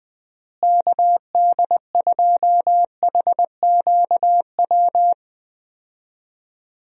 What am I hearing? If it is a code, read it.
Morse code, heard as a keyed tone, about 21 wpm, 700 Hz. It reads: KD2HQW
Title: USA Call Signs - 6 Characters - Repeated 2x - 20wpm